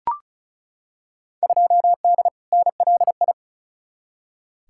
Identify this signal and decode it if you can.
Morse code, heard as a keyed tone, about 34 wpm, 700 Hz. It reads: 2D NLI
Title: Field Day Exchanges - 35wpm